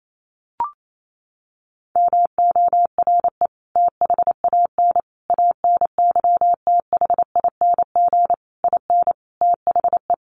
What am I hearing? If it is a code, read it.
Morse code, heard as a keyed tone, about 28 wpm, 700 Hz. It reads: MORE THAN ANYTHING IN THE
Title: Sentences from Top 1000 Words 28wpm